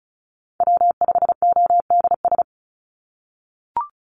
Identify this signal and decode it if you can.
Morse code, heard as a keyed tone, about 34 wpm, 700 Hz. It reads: W5ODS
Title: NA SOTA Activator Callsigns 35wpm